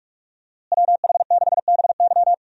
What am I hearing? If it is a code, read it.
Morse code, heard as a keyed tone, about 42 wpm, 700 Hz. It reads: WH6BY
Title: USA Call Signs - 5 Characters - Repeated 2x - 45wpm